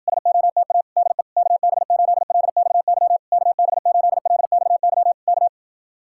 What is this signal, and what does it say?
Morse code, heard as a keyed tone, about 59 wpm, 700 Hz. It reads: SOTA DE KB8LXY KB8LXY K